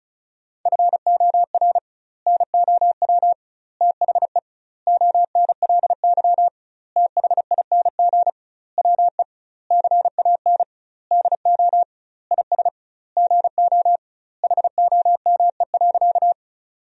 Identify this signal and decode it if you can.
Morse code, heard as a keyed tone, about 36 wpm, 700 Hz. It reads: FOR NOW THE ONLY THING WE CAN DO IS GO HOME.
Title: 2x Word-Spacing - Sentences from Top 200 Words - 35wpm